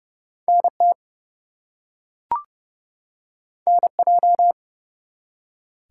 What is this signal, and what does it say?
Morse code, heard as a keyed tone, about 30 wpm, 700 Hz. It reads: NT NJ
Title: US States and Canadian Provinces 30wpm